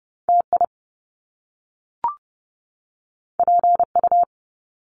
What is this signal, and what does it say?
Morse code, heard as a keyed tone, about 30 wpm, 700 Hz. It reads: TI PU
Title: Top Two Letter Combinations in Words 30wpm